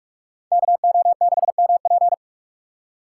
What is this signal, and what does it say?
Morse code, heard as a keyed tone, about 56 wpm, 700 Hz. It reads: KO6MP
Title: Speed Racing - USA 2x2 Call Signs 45wpm